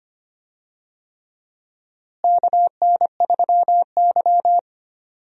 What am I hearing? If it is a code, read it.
Morse code, heard as a keyed tone, about 25 wpm, 700 Hz. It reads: KN3Y
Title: USA Call Signs - 4 Characters - Repeated 3x - 25wpm